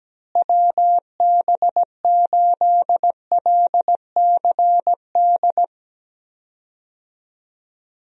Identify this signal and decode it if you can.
Morse code, heard as a keyed tone, about 17 wpm, 700 Hz. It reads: WB8LCD
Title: USA Call Signs - 6 Characters - Repeated 2x - 17wpm